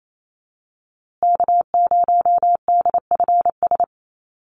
Morse code, heard as a keyed tone, about 28 wpm, 700 Hz. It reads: K0DFS
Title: USA Call Signs Repeated 3x 28wpm